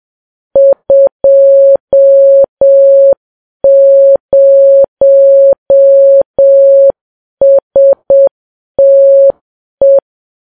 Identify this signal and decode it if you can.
Morse code, heard as a keyed tone, about 7 wpm, 550 Hz. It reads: 20STE
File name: Files Web tls / Morse on9fv / morse7kps